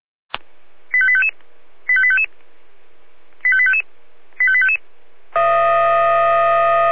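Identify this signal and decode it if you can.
Five-tone selective call (ZVEI), heard as quick runs of five tones, tones of about 70 ms. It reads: 85855 85855 85855 85855
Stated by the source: ZVEI five-tone selective call 85320 received on SDR.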